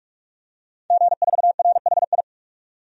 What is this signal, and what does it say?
Morse code, heard as a keyed tone, about 46 wpm, 700 Hz. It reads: G4RHI
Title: DXCC-100 Call Signs - 5 Characters - 45wpm